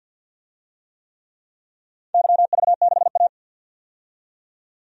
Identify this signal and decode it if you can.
Morse code, heard as a keyed tone, about 52 wpm, 700 Hz. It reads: YV6A
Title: DXCC-100 Call Signs - 4 Characters - 50wpm